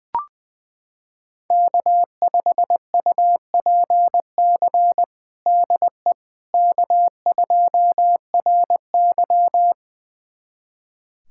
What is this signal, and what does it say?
Morse code, heard as a keyed tone, about 20 wpm, 700 Hz. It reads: K5UPC DE K2RY
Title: USA Callsign DE Callsign 20wpm